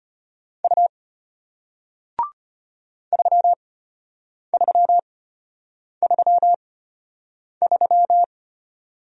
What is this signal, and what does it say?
Morse code, heard as a keyed tone, about 33 wpm, spaced as for 8 wpm, 700 Hz. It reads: U 3333